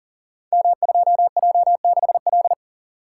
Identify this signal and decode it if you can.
Morse code, heard as a keyed tone, about 42 wpm, 700 Hz. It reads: M2J6L
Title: Code Groups - 5 Letter:Numbers - 40wpm